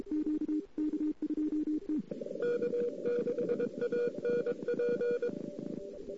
Morse code, heard as a keyed tone, about 22 wpm, 480 Hz. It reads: K6ANP
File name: CW / 2008-ss-cw-k5tr-op-n5rz -02:00